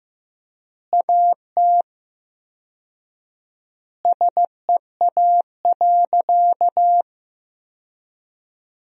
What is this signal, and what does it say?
Morse code, heard as a keyed tone, about 16 wpm, 700 Hz. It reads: AT SEA.
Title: 4x Word-Spacing - Sentences from Top 300 Words - 15wpm